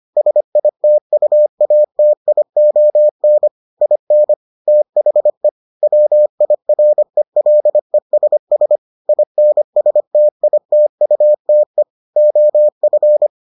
Morse code, heard as a keyed tone, about 25 wpm, 600 Hz. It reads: SITUATION IN THE WIRELESS INSTITUTE OF